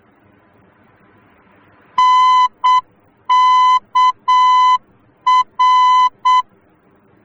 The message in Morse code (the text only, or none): NKR